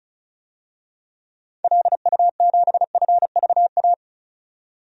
Morse code, heard as a keyed tone, about 34 wpm, 700 Hz. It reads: LU7FVA
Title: DXCC-100 Call Signs - 6 Characters - 35wpm